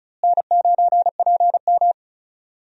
Morse code, heard as a keyed tone, about 34 wpm, 700 Hz. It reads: N9PM
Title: USA Call Signs - 4 Characters - Repeated 2x - 35wpm